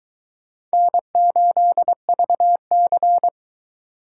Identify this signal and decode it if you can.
Morse code, heard as a keyed tone, about 22 wpm, 700 Hz. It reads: N8VC